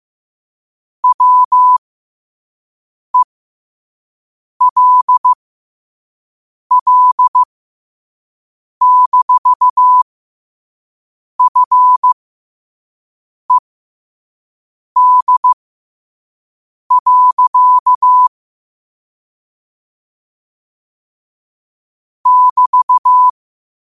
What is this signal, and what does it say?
Morse code, heard as a keyed tone, about 15 wpm, spaced as for 5 wpm, 1000 Hz. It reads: WELL-FED. <BT>
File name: qotdfiles / 5wpm